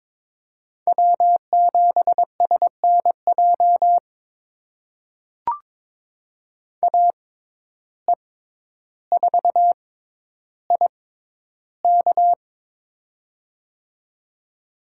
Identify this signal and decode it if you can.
Morse code, heard as a keyed tone, about 22 wpm, spaced as for 11 wpm, 700 Hz. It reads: W7SNJ AE4IK
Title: USA Call Signs - 5 Characters - 6x Character Spacing - 22wpm